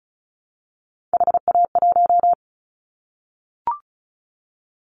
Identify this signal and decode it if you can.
Morse code, heard as a keyed tone, about 36 wpm, 700 Hz. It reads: HA1
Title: Callsign Prefixes for Top 25 DXCC - 35wpm